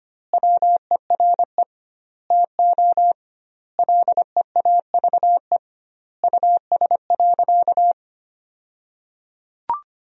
Morse code, heard as a keyed tone, about 25 wpm, 700 Hz. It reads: WERE TO LEAVE US.
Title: 2x Word-Spacing - Sentences from Top 400 Words - 25wpm